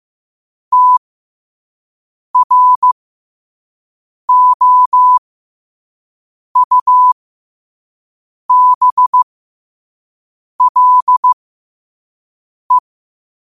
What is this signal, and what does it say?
Morse code, heard as a keyed tone, about 15 wpm, spaced as for 5 wpm, 1000 Hz. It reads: TROUBLE
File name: qotdfiles / 5wpm